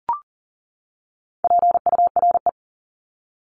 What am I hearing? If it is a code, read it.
Morse code, heard as a keyed tone, about 40 wpm, 700 Hz. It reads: PURE
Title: LY Suffix - Top 100 Words with Base Words 40wpm